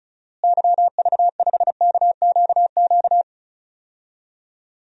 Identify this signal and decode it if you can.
Morse code, heard as a keyed tone, about 36 wpm, 700 Hz. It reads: YV5KQQ
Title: DXCC-100 Call Signs - 6 Characters - Repeated 3x - 35wpm